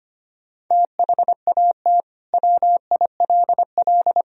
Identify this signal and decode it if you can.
Morse code, heard as a keyed tone, about 25 wpm, 700 Hz. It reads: THAT WILL
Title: Sentences from Top 100 Words and Abbreviations 25wpm